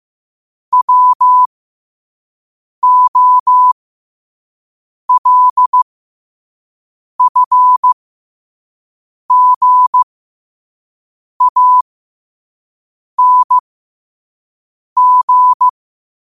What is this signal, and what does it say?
Morse code, heard as a keyed tone, about 15 wpm, spaced as for 5 wpm, 1000 Hz. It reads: WOLFGANG
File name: qotdfiles / 5wpm